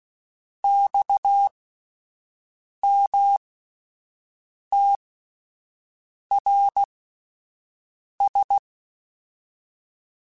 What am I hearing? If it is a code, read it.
Morse code, heard as a keyed tone, about 16 wpm, spaced as for 5 wpm, 800 Hz. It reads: XMTRS
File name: (spaced) / QSOs Spaced-mp3s / WA7QOX 1 @10 sp